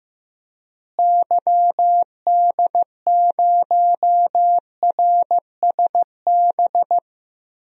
Morse code, heard as a keyed tone, about 15 wpm, 700 Hz. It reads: YD0RSB
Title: DXCC-100 Call Signs - 6 Characters - Repeated 3x - 15wpm